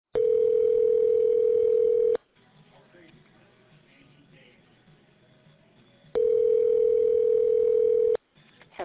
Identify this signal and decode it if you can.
Ringback tone (North America), 440 + 480 Hz.